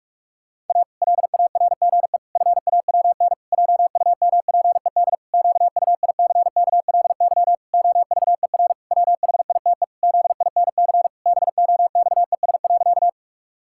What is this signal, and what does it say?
Morse code, heard as a keyed tone, about 46 wpm, 700 Hz. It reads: A LARGE FAWN JUMPED QUICKLY OVER WHITE ZINC BOXES.
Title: Pangram Challenge v2 - 45wpm